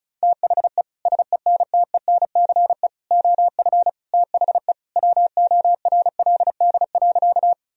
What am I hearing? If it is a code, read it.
Morse code, heard as a keyed tone, about 34 wpm, 700 Hz. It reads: THE SENTENCE OF THE WORLD.
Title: Sentences from Top 1000 Words - Review Part 1 - 35wpm